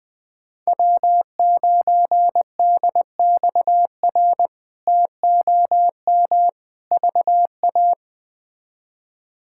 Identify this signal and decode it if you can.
Morse code, heard as a keyed tone, about 21 wpm, 700 Hz. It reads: W9DXR TOM VA
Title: K1USN SST QSO Practice - 20wpm